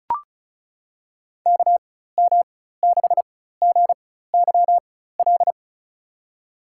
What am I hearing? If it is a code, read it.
Morse code, heard as a keyed tone, about 34 wpm, spaced as for 16 wpm, 700 Hz. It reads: KM6GYL
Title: USA Call Signs - 4x Character Spacing - 35wpm